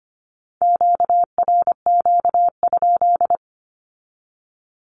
Morse code, heard as a keyed tone, about 25 wpm, 700 Hz. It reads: QRQ?
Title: Q-Codes with Spoken Meaning - 25wpm